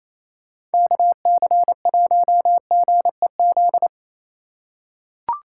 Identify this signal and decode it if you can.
Morse code, heard as a keyed tone, about 28 wpm, 700 Hz. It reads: KC1GEZ
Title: USA Call Signs - 6 Characters - 2x Character Spacing - 28wpm